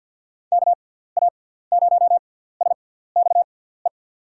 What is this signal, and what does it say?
Morse code, heard as a keyed tone, about 51 wpm, spaced as for 17 wpm, 700 Hz. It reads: KA0SXE
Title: USA Call Signs - 6 Characters - 6x Character Spacing - 50wpm